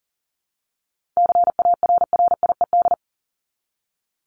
Morse code, heard as a keyed tone, about 40 wpm, 700 Hz. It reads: CARRIED
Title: ED Suffix - Top 100 Words with Base Words 40wpm